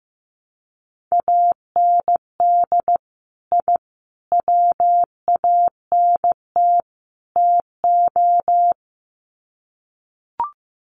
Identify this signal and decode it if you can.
Morse code, heard as a keyed tone, about 15 wpm, 700 Hz. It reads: AND I WANT TO
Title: Sets of 4 Words 15wpm